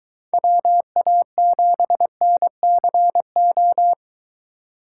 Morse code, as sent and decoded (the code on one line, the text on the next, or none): .-- .- --... -. -.-. ---
WA7NCO